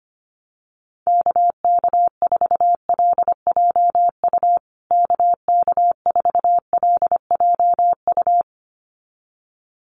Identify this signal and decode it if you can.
Morse code, heard as a keyed tone, about 25 wpm, 700 Hz. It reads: KK4LJU KK4LJU
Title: USA Call Signs Repeated 2x 25wpm